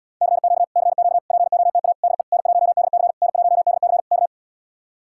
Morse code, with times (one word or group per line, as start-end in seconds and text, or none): 0.2-0.6 CQ
0.8-1.2 CQ
1.3-1.9 POTA
2.0-2.2 DE
2.3-3.1 N1DC
3.2-4.0 N1DC
4.1-4.3 K